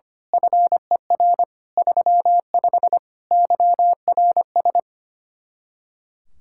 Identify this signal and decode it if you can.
Morse code, heard as a keyed tone, about 24 wpm, 700 Hz. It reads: FER 35 YRS
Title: Speed-Racing Common QSO Phrases - 25wpm